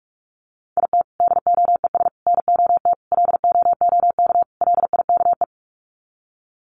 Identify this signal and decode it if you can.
Morse code, heard as a keyed tone, about 44 wpm, 700 Hz. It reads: IT DOES NOT LOOK LIKE
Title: Sets of 5 Words Encore2 45wpm